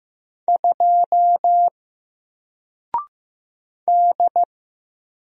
Morse code, heard as a keyed tone, about 15 wpm, 700 Hz. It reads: 2 D
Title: Single Character (Letter:Number:Punctuation:Prosign) Alt 15wpm